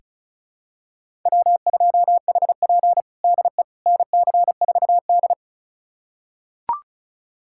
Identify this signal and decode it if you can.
Morse code, heard as a keyed tone, about 36 wpm, 700 Hz. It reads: W2HP DE NC4D
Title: USA Callsign DE Callsign - 4 Characters - 35wpm